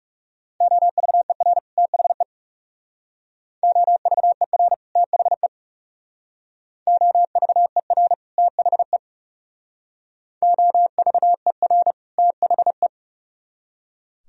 Morse code, heard as a keyed tone, about 36 wpm, 700 Hz. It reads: OVER THE OVER THE OVER THE OVER THE
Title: Speed-Racing - Sets of 2 Words 30wpm